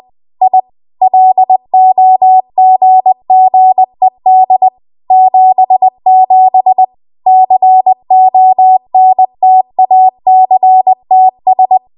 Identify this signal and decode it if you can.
Morse code, heard as a keyed tone, about 20 wpm, 750 Hz. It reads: I LOGGED 77 CONTACTS
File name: Archive / 20 WPM / 20WPM